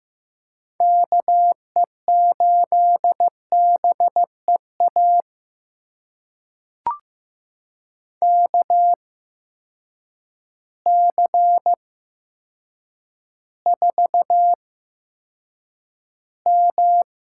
Morse code, heard as a keyed tone, about 15 wpm, spaced as for 8 wpm, 700 Hz. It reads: KE8BEA KC4M
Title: USA Call Signs - 6 Characters - 8x Character Spacing - 15wpm